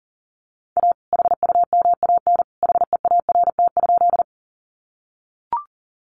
Morse code, heard as a keyed tone, about 40 wpm, 700 Hz. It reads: A HUMAN HEART?
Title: Sentences from Top 1200 Words 40wpm